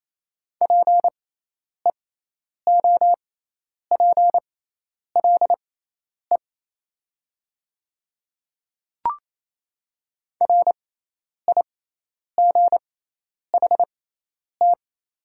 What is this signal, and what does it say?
Morse code, heard as a keyed tone, about 28 wpm, spaced as for 10 wpm, 700 Hz. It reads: PEOPLE RIGHT
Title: 6x Character Spacing - Top 200 Words - 28wpm